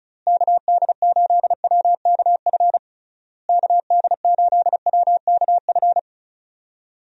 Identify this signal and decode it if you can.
Morse code, heard as a keyed tone, about 36 wpm, 700 Hz. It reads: KD8WKF KD8WKF
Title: USA Call Signs - 6 Characters - Repeated 3x - 35wpm